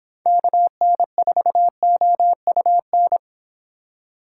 Morse code, heard as a keyed tone, about 27 wpm, 700 Hz. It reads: KN4OUN
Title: Speed Racing - USA 2x3 Call Signs 22wpm